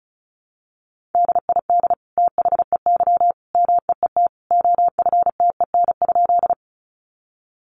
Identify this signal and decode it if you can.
Morse code, heard as a keyed tone, about 34 wpm, 700 Hz. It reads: DID THEY MEET OFTEN?
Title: Sentences from Top 1100 Words 35wpm